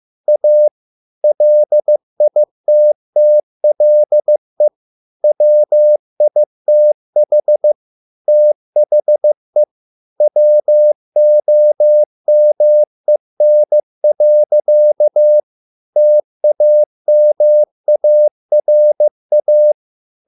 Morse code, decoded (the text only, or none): A LITTLE WITH THE WOMEN. TAMARA